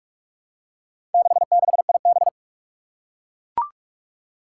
Morse code, heard as a keyed tone, about 44 wpm, 700 Hz. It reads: 66IB
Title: Four Challenging Letter-Number Combinations 45wpm